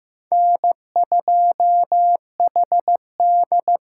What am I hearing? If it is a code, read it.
Morse code, heard as a keyed tone, about 15 wpm, 700 Hz. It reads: N2HD
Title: USA Call Signs - 5 Characters - 8x Character Spacing - 15wpm